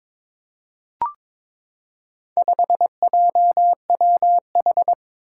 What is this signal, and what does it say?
Morse code, heard as a keyed tone, about 22 wpm, 700 Hz. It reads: 5JWH
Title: Four Challenging Letter-Number Combinations 22wpm